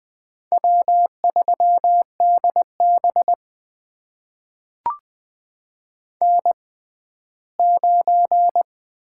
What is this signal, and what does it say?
Morse code, heard as a keyed tone, about 20 wpm, spaced as for 14 wpm, 700 Hz. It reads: W3DB N9
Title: USA Call Signs - 6x Character Spacing - 20wpm